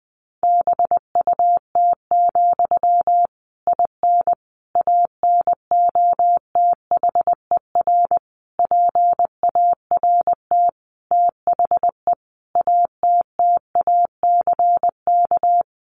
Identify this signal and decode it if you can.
Morse code, heard as a keyed tone, about 20 wpm, 700 Hz. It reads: BUT, IN ANOTHER PART THE ATTACK
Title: Sentences from Top 1500 Words 20wpm